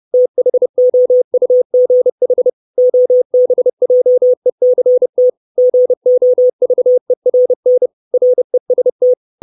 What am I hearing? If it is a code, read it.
Morse code, heard as a keyed tone, about 30 wpm, 500 Hz. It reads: THOUGH OBJECT GOVERN REST
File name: 632 Words 30WPM